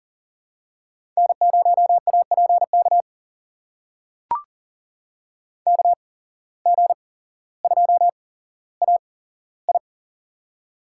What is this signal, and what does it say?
Morse code, heard as a keyed tone, about 40 wpm, spaced as for 17 wpm, 700 Hz. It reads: N0APK KG2AI
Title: USA Call Signs - 5 Characters - 8x Character Spacing - 40wpm